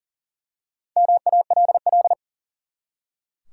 Morse code, heard as a keyed tone, about 39 wpm, 700 Hz. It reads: MALL
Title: Four Letter Words - 40wpm